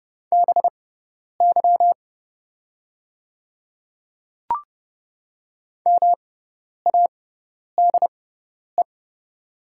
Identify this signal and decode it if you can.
Morse code, heard as a keyed tone, about 30 wpm, spaced as for 10 wpm, 700 Hz. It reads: BY MADE
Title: 6x Character Spacing - Top 300 Words - 30wpm